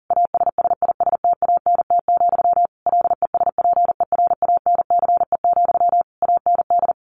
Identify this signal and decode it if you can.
Morse code, heard as a keyed tone, about 40 wpm, 700 Hz. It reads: ASSISTANT, LESPERANCE, AND
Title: A Sound Of Thunder 40wpm